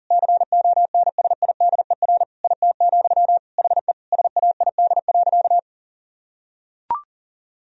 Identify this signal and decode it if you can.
Morse code, heard as a keyed tone, about 37 wpm, 700 Hz. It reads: CONSIDER IT, HE SAID.
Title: Sentences from Top 1000 Words - Review Part 4 - 40wpm